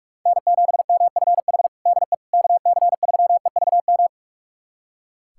Morse code, heard as a keyed tone, about 42 wpm, 700 Hz. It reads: N7MFH DE KC3EVM
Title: USA Callsign DE Callsign 45wpm